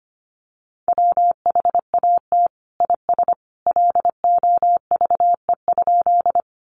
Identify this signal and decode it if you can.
Morse code, heard as a keyed tone, about 25 wpm, 700 Hz. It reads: WHAT IS LOVE?
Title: Sentences from Top 400 Words 25wpm